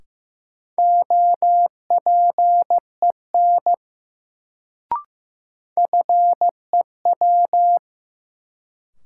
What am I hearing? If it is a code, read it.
Morse code, heard as a keyed tone, about 15 wpm, 700 Hz. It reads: OPEN FEW
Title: Sentences from Top 300 Words - New Word Drill - 15wpm